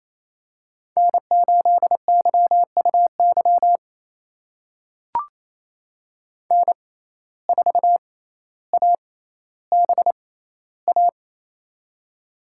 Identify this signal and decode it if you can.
Morse code, heard as a keyed tone, about 28 wpm, spaced as for 14 wpm, 700 Hz. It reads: N8YUY N4ABA
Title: USA Call Signs - 5 Characters - 6x Character Spacing - 28wpm